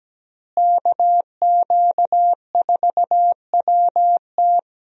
Morse code, heard as a keyed tone, about 17 wpm, 700 Hz. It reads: KQ4WT
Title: USA Call Signs - 5 Characters - Repeated 3x - 17wpm